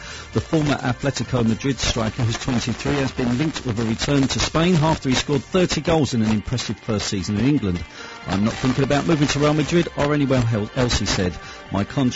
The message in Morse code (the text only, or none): none